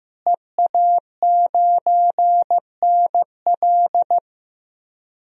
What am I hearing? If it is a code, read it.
Morse code, heard as a keyed tone, about 15 wpm, 700 Hz. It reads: EA9NL
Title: DXCC-100 Call Signs - 5 Characters - Repeated 3x - 15wpm